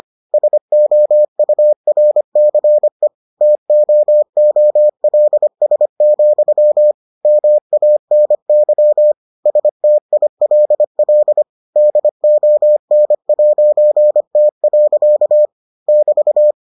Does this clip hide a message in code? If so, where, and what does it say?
Yes: Morse code from 0.3 to 16.5 s, heard as a keyed tone, about 25 wpm, 600 Hz. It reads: SOURCE TOOLS, MANY STILL DON'T. <BT>